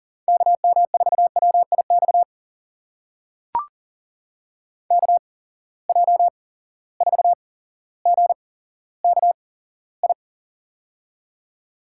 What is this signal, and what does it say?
Morse code, heard as a keyed tone, about 40 wpm, spaced as for 17 wpm, 700 Hz. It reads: KM4WIX KJ4GKI
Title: USA Call Signs - 6 Characters - 8x Character Spacing - 40wpm